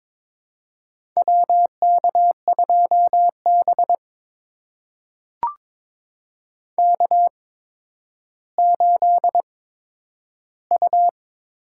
Morse code, heard as a keyed tone, about 22 wpm, spaced as for 11 wpm, 700 Hz. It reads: WK2B K8U
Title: USA Call Signs - 4 Characters - 8x Character Spacing - 22wpm